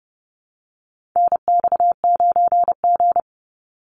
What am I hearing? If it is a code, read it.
Morse code, heard as a keyed tone, about 30 wpm, 700 Hz. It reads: NX9G
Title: Call Signs - 30wpm